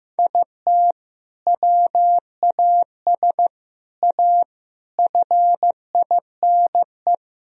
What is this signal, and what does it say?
Morse code, heard as a keyed tone, about 15 wpm, 700 Hz. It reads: IT WAS A FINE
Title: Sentences from Top 1000 Words - Review Part 3 - 15wpm